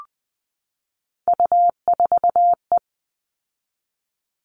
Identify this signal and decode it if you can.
Morse code, heard as a keyed tone, about 20 wpm, 700 Hz. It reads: U4E
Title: All Cut Numbers - 3-Digit 20wpm